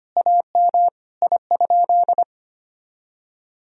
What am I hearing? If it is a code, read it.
Morse code, heard as a keyed tone, about 25 wpm, 700 Hz. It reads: AM I?
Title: Sentences from Top 1000 Words - Review Part 2 - 25wpm